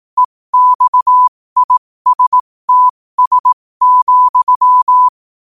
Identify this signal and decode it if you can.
Morse code, heard as a keyed tone, about 18 wpm, spaced as for 15 wpm, 1000 Hz. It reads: EXISTS,
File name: qotdfiles / 15wpm